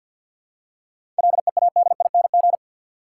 Morse code, heard as a keyed tone, about 52 wpm, 700 Hz. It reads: LEADING
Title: Seven Letter Words - 50wpm